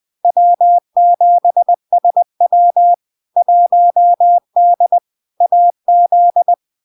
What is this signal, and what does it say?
Morse code, heard as a keyed tone, about 20 wpm, 700 Hz. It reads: W7SW 1D AZ
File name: LARK-FD-2022-Log-in-CW-20wpm-700-Hz